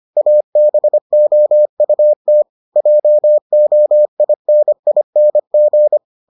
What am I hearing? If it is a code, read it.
Morse code, heard as a keyed tone, about 25 wpm, 600 Hz. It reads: ABOUT JOINING